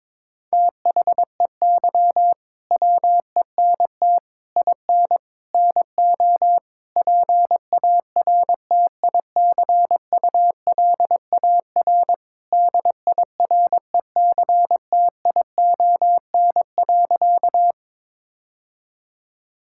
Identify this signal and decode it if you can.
Morse code, heard as a keyed tone, about 22 wpm, 700 Hz. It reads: THEY WENT IN NO PARTICULAR DIRECTION.
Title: Sentences from Top 1000 Words - Review Part 5 - 22wpm